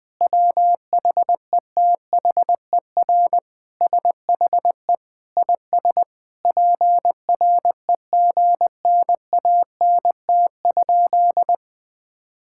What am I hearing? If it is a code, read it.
Morse code, heard as a keyed tone, about 20 wpm, 700 Hz. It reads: WHETHER SHE IS PREGNANT?
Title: Sentences from Top 1000 Words - Review Part 6 - 20wpm